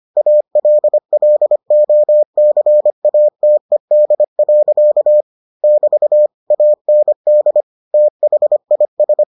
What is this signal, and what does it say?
Morse code, heard as a keyed tone, about 25 wpm, 600 Hz. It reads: ALLOCATED. <BT> AND THIS